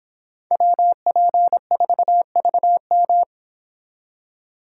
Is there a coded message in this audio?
Yes: Morse code from 0.5 to 3.2 s, heard as a keyed tone, about 26 wpm, 700 Hz. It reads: WP4VM